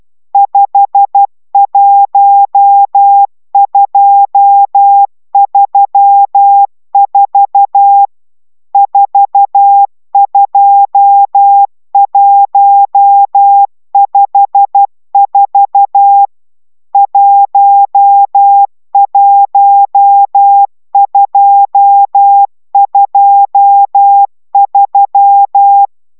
Morse code, heard as a keyed tone, about 12 wpm, 800 Hz. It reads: 51234 42154 11223